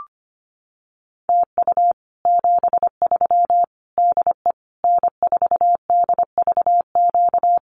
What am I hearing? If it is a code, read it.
Morse code, heard as a keyed tone, about 25 wpm, 700 Hz. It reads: TU 73 DE N4DVQ
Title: Common QSO Phrases Encore 25wpm